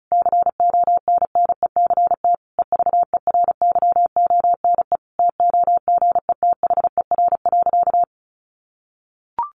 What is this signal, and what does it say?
Morse code, heard as a keyed tone, about 34 wpm, 700 Hz. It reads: CONNECT EVERYONE TOGETHER.
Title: Sentences from Top 1500 Words 35wpm